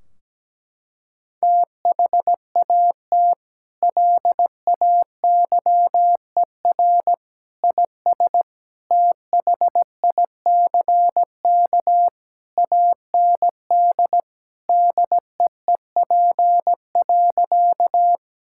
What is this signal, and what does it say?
Morse code, heard as a keyed tone, about 17 wpm, 700 Hz. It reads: THAT LAYER IS THICK AND DEEP.
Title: Sentences from Top 1600 Words 17wpm